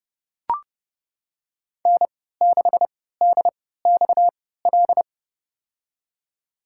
Morse code, heard as a keyed tone, about 30 wpm, spaced as for 17 wpm, 700 Hz. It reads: N6DXL
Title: USA Call Signs - 5 Characters - 3x Character Spacing - 30wpm